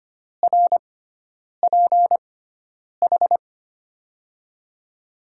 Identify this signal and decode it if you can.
Morse code, heard as a keyed tone, about 25 wpm, spaced as for 9 wpm, 700 Hz. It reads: RPH
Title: USA Call Signs - 6 Characters - 6x Character Spacing - 25wpm